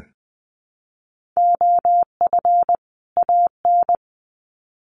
Morse code, heard as a keyed tone, about 20 wpm, 700 Hz. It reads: OF AN
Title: Sets of 2 Words 20wpm